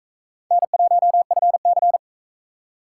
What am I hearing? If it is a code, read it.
Morse code, heard as a keyed tone, about 43 wpm, 700 Hz. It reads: N1FC